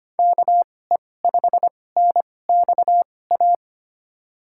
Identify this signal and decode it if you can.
Morse code, heard as a keyed tone, about 25 wpm, spaced as for 18 wpm, 700 Hz. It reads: KE5NXA
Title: USA Call Signs - 6 Characters - 2x Character Spacing - 25wpm